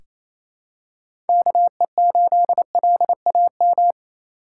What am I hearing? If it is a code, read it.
Morse code, heard as a keyed tone, about 27 wpm, 700 Hz. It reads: KE8LAM
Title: USA Call Signs - 6 Characters - 6x Character Spacing - 28wpm